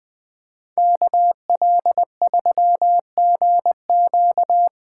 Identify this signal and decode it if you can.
Morse code, heard as a keyed tone, about 24 wpm, 700 Hz. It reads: KL3GQ
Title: Speed Racing - USA 2x2 Call Signs 20wpm